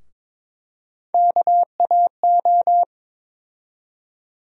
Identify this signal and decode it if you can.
Morse code, heard as a keyed tone, about 22 wpm, 700 Hz. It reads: KAO
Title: USA Call Sign Postfixes - 3 Letters 22wpm